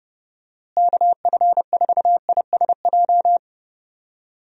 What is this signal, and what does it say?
Morse code, heard as a keyed tone, about 32 wpm, 700 Hz. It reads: KF4ISJ
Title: USA Call Signs - 6 Characters - Repeated 3x - 30wpm